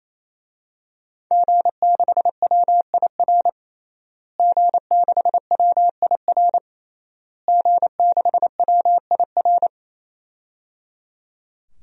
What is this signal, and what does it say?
Morse code, heard as a keyed tone, about 27 wpm, 700 Hz. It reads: G6WIR G6WIR G6WIR
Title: DXCC-100 Call Signs - 5 Characters - Repeated 3x - 28wpm